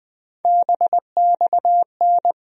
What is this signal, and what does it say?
Morse code, heard as a keyed tone, about 20 wpm, 700 Hz. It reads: BXN
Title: USA Call Sign Postfixes - 3 Letters 20wpm